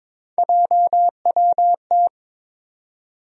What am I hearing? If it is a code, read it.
Morse code, heard as a keyed tone, about 22 wpm, 700 Hz. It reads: JWT
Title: USA Call Sign Postfixes - 3 Letters 22wpm